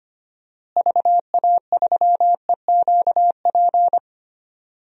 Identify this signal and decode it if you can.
Morse code, heard as a keyed tone, about 25 wpm, 700 Hz. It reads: VA3EQP
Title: DXCC-100 Call Signs - 6 Characters - Repeated 3x - 25wpm